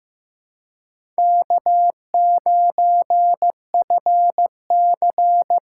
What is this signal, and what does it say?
Morse code, heard as a keyed tone, about 15 wpm, 700 Hz. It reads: K9FC
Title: USA Call Signs - 4 Characters - 2x Character Spacing - 15wpm